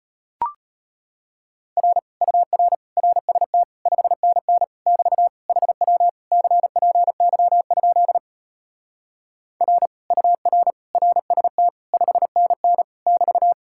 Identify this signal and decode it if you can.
Morse code, heard as a keyed tone, about 37 wpm, 700 Hz. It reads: R UR RST 5NN <BT> HW CPY? R UR RST 5NN <BT>